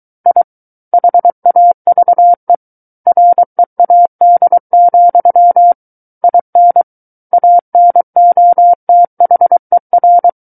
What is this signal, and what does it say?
Morse code, heard as a keyed tone, about 23 wpm, 700 Hz. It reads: I HAVE READ, IN ANOTHER
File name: Book One / 14-Chapter 14 - 23 WPM